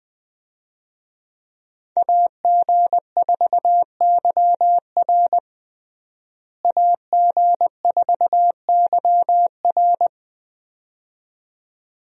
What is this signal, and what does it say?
Morse code, heard as a keyed tone, about 20 wpm, 700 Hz. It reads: AG4YR AG4YR
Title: USA Call Signs - 5 Characters - Repeated 2x - 20wpm